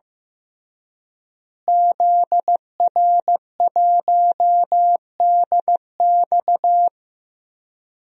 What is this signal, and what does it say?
Morse code, heard as a keyed tone, about 15 wpm, 700 Hz. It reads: ZR1DX
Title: DXCC-100 Call Signs - 5 Characters - Repeated 3x - 15wpm